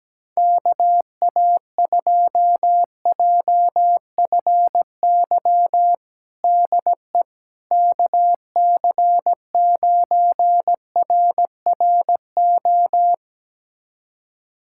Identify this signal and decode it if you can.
Morse code, heard as a keyed tone, about 17 wpm, 700 Hz. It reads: KA2JFY DE KC9RRO
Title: USA Callsign DE Callsign - 6 Characters - 17wpm